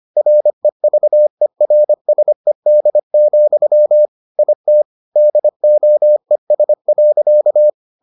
Morse code, heard as a keyed tone, about 25 wpm, 600 Hz. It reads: REVERSED, IT DOES.